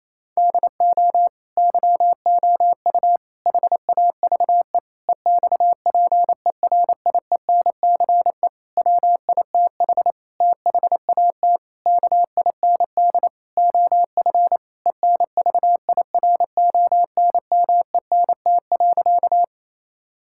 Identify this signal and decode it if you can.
Morse code, heard as a keyed tone, about 27 wpm, 700 Hz. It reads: DO YOU HAVE EXPERIENCE WITH THAT KIND OF ENVIRONMENT.
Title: Sentences from Top 1000 Words - Review Part 5 - 28wpm